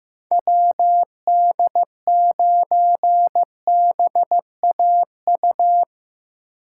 Morse code, heard as a keyed tone, about 15 wpm, 700 Hz. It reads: WD9BAU
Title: USA Call Signs - 6 Characters - 2x Character Spacing - 15wpm